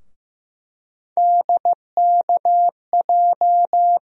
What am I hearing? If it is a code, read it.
Morse code, heard as a keyed tone, about 15 wpm, 700 Hz. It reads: DKJ
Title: USA Call Sign Postfixes - 3 Letters 15wpm